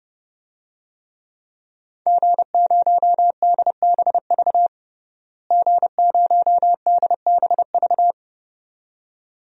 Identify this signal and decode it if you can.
Morse code, heard as a keyed tone, about 30 wpm, 700 Hz. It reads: G0DBV G0DBV
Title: DXCC-100 Call Signs - 5 Characters - Repeated 2x - 30wpm